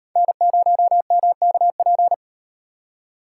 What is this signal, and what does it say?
Morse code, heard as a keyed tone, about 38 wpm, 700 Hz. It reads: N0MKP